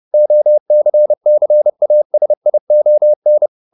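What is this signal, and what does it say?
Morse code, heard as a keyed tone, about 30 wpm, 600 Hz. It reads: OCCASION
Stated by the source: Morse code at 30 wpm, the novel Never by Ken Follett in English, from CW med Gnister.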